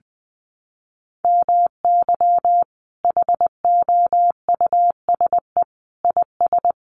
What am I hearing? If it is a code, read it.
Morse code, heard as a keyed tone, about 20 wpm, 700 Hz. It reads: MY HOUSE IS
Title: Sentences from Top 1100 Words 20wpm